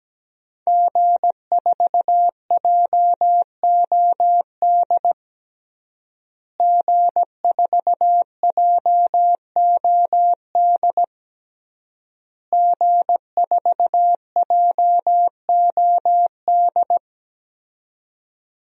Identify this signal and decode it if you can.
Morse code, heard as a keyed tone, about 17 wpm, 700 Hz. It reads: G4JOD G4JOD G4JOD
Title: DXCC-100 Call Signs - 5 Characters - Repeated 3x - 17wpm